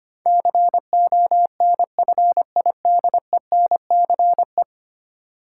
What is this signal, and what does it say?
Morse code, heard as a keyed tone, about 25 wpm, 700 Hz. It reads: CONFIDENCE
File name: Ten Letter Words - 25wpm